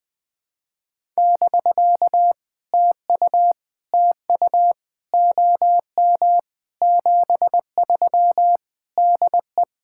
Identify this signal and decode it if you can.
Morse code, heard as a keyed tone, about 20 wpm, 700 Hz. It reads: <BK> TU TU OM 73 DE
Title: POTA QSO Practice - 20wpm